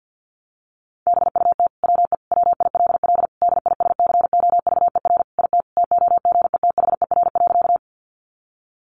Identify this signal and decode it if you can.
Morse code, heard as a keyed tone, about 50 wpm, 700 Hz. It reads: BUT WE WILL DISCOVER IT TOGETHER.
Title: Sentences from Top 1100 Words 50wpm